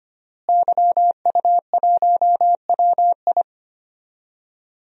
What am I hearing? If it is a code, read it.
Morse code, heard as a keyed tone, about 25 wpm, 700 Hz. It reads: YU1WI
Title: DXCC-100 Call Signs - 5 Characters - Repeated 2x - 25wpm